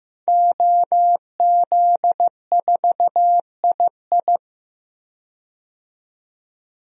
Morse code, heard as a keyed tone, about 15 wpm, 700 Hz. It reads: OZ4II
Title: DXCC-100 Call Signs - 5 Characters - 15wpm